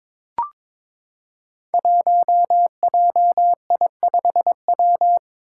Morse code, heard as a keyed tone, about 22 wpm, 700 Hz. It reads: 1JI5W
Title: Five Challenging Letter-Number Combinations 22wpm